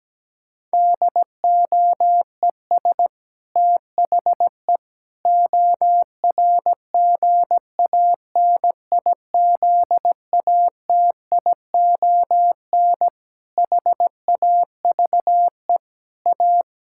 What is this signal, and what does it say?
Morse code, heard as a keyed tone, about 17 wpm, 700 Hz. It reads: DOES THE ORGANIZATION HAVE A
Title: Sentences from Top 1000 Words - Review Part 6 - 17wpm